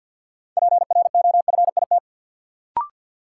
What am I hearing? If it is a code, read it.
Morse code, heard as a keyed tone, about 52 wpm, 700 Hz. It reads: PROFIT
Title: Sentences from Top 800 Words - New Word Drill - 50wpm